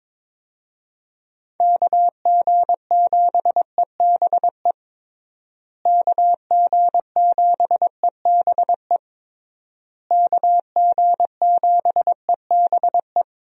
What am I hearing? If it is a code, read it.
Morse code, heard as a keyed tone, about 22 wpm, 700 Hz. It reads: KG7EBE KG7EBE KG7EBE
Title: USA Call Signs - 6 Characters - Repeated 3x - 22wpm